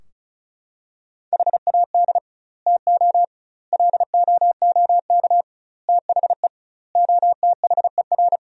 Morse code, heard as a keyed tone, about 36 wpm, 700 Hz. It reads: HAD TO LOOK THE OTHER
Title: 2x Word-Spacing - Sentences from Top 100 Words - 35wpm